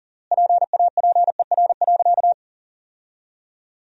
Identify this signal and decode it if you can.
Morse code, heard as a keyed tone, about 40 wpm, 700 Hz. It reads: PAPER.
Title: 3x Word-Spacing - Sentences from Top 300 Words - 40wpm